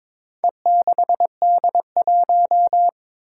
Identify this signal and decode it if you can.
Morse code, heard as a keyed tone, about 22 wpm, 700 Hz. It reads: E6D1
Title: Four Challenging Letter-Number Combinations 22wpm